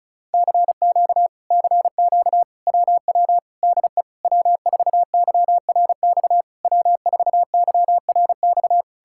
Morse code, heard as a keyed tone, about 37 wpm, 700 Hz. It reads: CQ CQ WW DE W4YRX W4YRX
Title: Speed-Racing Common QSO Phrases Encore - 30wpm